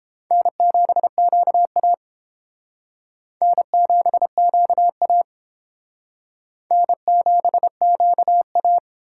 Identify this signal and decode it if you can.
Morse code, heard as a keyed tone, about 30 wpm, 700 Hz. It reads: N7QA N7QA N7QA